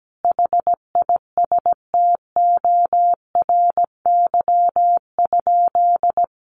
Morse code, heard as a keyed tone, about 17 wpm, 700 Hz. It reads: HISTORY?
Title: Sentences from Top 600 Words 17wpm